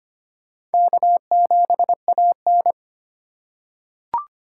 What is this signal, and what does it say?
Morse code, heard as a keyed tone, about 25 wpm, 700 Hz. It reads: K7AN
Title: USA Call Signs - 4 Characters - 4x Character Spacing - 25wpm